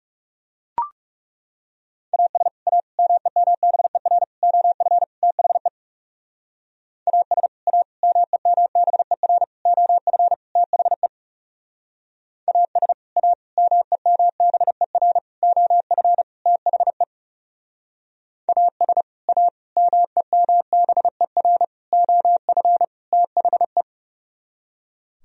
Morse code, heard as a keyed tone, about 40 wpm, 700 Hz. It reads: AS A MEMBER OF THE AS A MEMBER OF THE AS A MEMBER OF THE AS A MEMBER OF THE